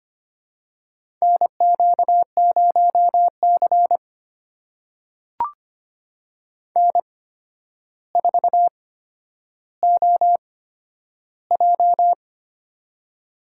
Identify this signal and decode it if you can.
Morse code, heard as a keyed tone, about 25 wpm, spaced as for 11 wpm, 700 Hz. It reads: NQ0C N4OJ
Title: USA Call Signs - 4 Characters - 8x Character Spacing - 25wpm